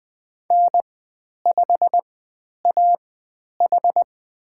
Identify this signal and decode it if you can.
Morse code, heard as a keyed tone, about 20 wpm, spaced as for 10 wpm, 700 Hz. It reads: N5AH
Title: 095 Lesson 16g - Letter c - Call Signs 20wpm